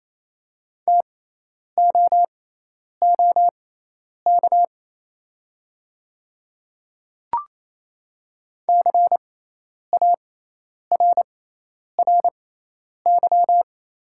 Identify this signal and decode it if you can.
Morse code, heard as a keyed tone, about 28 wpm, spaced as for 10 wpm, 700 Hz. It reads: TOOK CARRY
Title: 6x Character Spacing - Top 300 Words - 28wpm